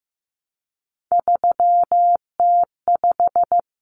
Morse code, heard as a keyed tone, about 15 wpm, 700 Hz. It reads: 3T5
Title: Common Cut Numbers - 3-Digit 15wpm